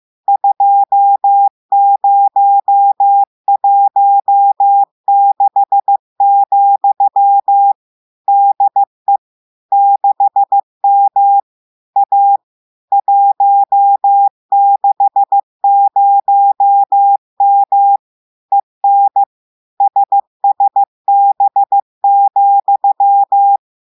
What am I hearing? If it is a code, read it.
Morse code, heard as a keyed tone, about 15 wpm, 800 Hz. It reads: 2016, DE 6M A 160M EN SSB,
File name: eh1ard-amateur-radio-day-2016 mf feed 1